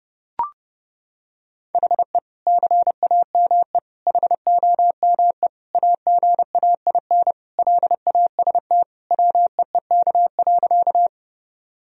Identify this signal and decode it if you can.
Morse code, heard as a keyed tone, about 30 wpm, 700 Hz. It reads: HE CAME HOME AGAIN LAST WEEK.
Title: Sentences from Top 1000 Words - Review Part 2 - 30wpm